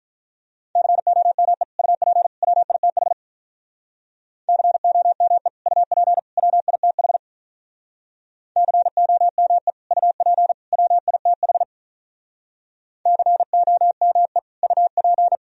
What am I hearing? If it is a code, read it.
Morse code, heard as a keyed tone, about 47 wpm, 700 Hz. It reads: COME UP WITH COME UP WITH COME UP WITH COME UP